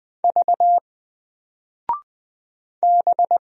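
Morse code, heard as a keyed tone, about 20 wpm, 700 Hz. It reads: V B
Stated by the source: Single letter and number Morse drill at 20 wpm, from N4LCD.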